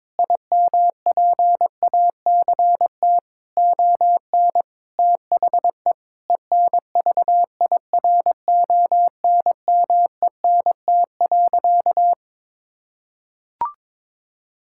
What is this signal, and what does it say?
Morse code, heard as a keyed tone, about 22 wpm, 700 Hz. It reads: IMPACT ON THE ENVIRONMENT.
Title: Sentences from Top 1000 Words - Review Part 6 - 22wpm